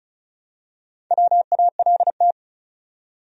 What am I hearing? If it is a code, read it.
Morse code, heard as a keyed tone, about 34 wpm, 700 Hz. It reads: WALT
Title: USA Common Names - Max 4 Letters - 35wpm